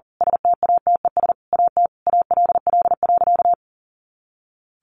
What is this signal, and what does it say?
Morse code, heard as a keyed tone, about 40 wpm, 700 Hz. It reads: STATES AT ALL.
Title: A Sound Of Thunder 40wpm